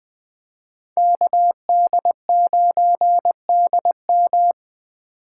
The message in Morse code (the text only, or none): KD9DM